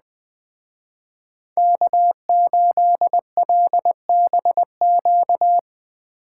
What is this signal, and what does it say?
Morse code, heard as a keyed tone, about 20 wpm, 700 Hz. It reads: K8LBQ
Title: Speed Racing - USA 1x3 Call Signs 17wpm